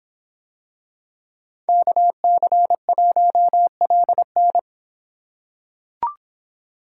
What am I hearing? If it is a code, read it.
Morse code, heard as a keyed tone, about 26 wpm, 700 Hz. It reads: KC1LN